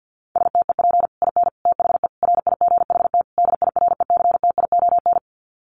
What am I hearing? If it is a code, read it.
Morse code, heard as a keyed tone, about 50 wpm, 700 Hz. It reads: STEP IN THE RIGHT DIRECTION
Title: Sets of 5 Words Encore 50wpm